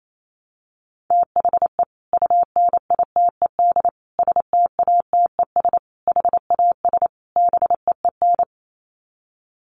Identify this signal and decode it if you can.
Morse code, heard as a keyed tone, about 28 wpm, 700 Hz. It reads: THE UNITED STATES HAS BEEN
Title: Sets of 5 Words Encore2 28wpm